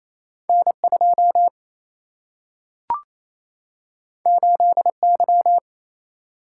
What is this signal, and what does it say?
Morse code, heard as a keyed tone, about 28 wpm, 700 Hz. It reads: N2 8Y
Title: Single-Letter Single-Number 28wpm